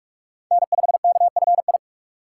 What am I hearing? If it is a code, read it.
Morse code, heard as a keyed tone, about 43 wpm, 700 Hz. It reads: N5KFI